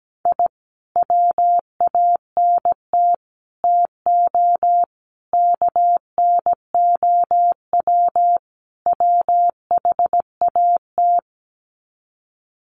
Morse code, heard as a keyed tone, about 17 wpm, 700 Hz. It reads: I WANT TO KNOW WHAT
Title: Sets of 5 Words Encore2 17wpm